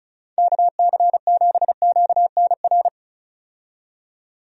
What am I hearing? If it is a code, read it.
Morse code, heard as a keyed tone, about 35 wpm, 700 Hz. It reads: KC7QNR
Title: USA Call Signs - 6 Characters - 35wpm